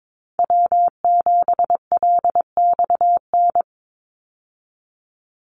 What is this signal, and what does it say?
Morse code, heard as a keyed tone, about 22 wpm, 700 Hz. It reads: W7LXN
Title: Call Signs Repeated 3x - 22wpm